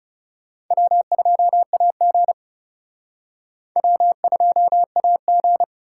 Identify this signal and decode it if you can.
Morse code, heard as a keyed tone, about 37 wpm, 700 Hz. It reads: W2AG W2AG
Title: Speed Racing - USA 1x2 Call Signs 30wpm